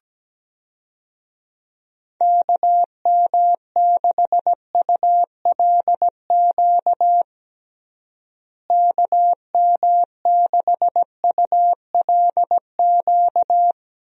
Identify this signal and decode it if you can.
Morse code, heard as a keyed tone, about 17 wpm, 700 Hz. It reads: KM6ULQ KM6ULQ
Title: USA Call Signs - 6 Characters - Repeated 3x - 17wpm